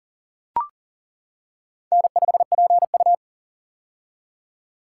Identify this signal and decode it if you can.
Morse code, heard as a keyed tone, about 40 wpm, 700 Hz. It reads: N5PU
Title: K1USN SST QSO Practice - 40wpm